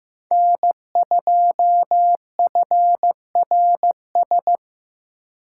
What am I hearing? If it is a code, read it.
Morse code, heard as a keyed tone, about 18 wpm, 700 Hz. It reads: N2FRS
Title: Speed Racing - USA 1x3 Call Signs 15wpm